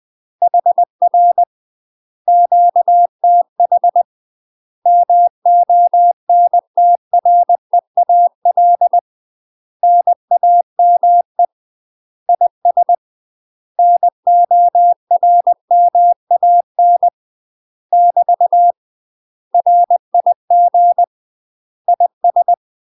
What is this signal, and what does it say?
Morse code, heard as a keyed tone, about 20 wpm, 700 Hz. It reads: HR QTH MONTREAL NAME IS NORMAN <BT> RIG IS
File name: UK MASTER QSOs TRAINING / 20WPM / UK MASTER QSOs 0046